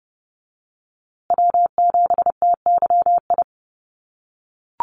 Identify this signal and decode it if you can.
Morse code, heard as a keyed tone, about 30 wpm, 700 Hz. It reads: W7TYI
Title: USA Call Signs Repeated 2x 30wpm